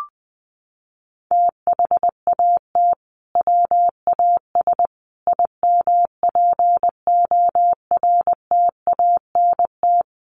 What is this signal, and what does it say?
Morse code, heard as a keyed tone, about 20 wpm, 700 Hz. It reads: THAT WAS IMPORTANT
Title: Sentences from Top 600 Words 20wpm